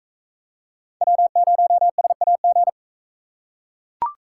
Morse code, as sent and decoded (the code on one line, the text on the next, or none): .-- ----- ... .- --.
W0SAG